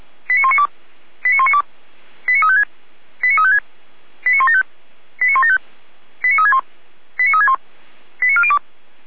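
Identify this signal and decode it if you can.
Five-tone selective call (ZVEI), heard as quick runs of five tones, tones of about 80 ms. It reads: 89192 89192 89356 89356 89175 89175 89361 89361 89402
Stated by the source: ZVEI five-tone call 89108 received by SDR.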